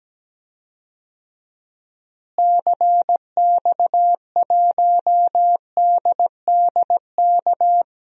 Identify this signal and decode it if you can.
Morse code, heard as a keyed tone, about 17 wpm, 700 Hz. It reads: CX1DDK
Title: DXCC-100 Call Signs - 6 Characters - Repeated 3x - 17wpm